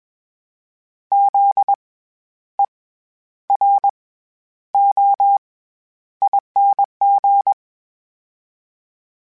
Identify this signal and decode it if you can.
Morse code, heard as a keyed tone, about 21 wpm, spaced as for 11 wpm, 800 Hz. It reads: ZEROING
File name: Special INGs-mp3s / Special INGs-mp3s / SlowToFaster ing @20